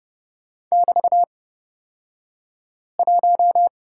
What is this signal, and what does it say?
Morse code, heard as a keyed tone, about 30 wpm, spaced as for 5 wpm, 700 Hz. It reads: <BT>1
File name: Lesson 36d - BT - Two Characters 30wpm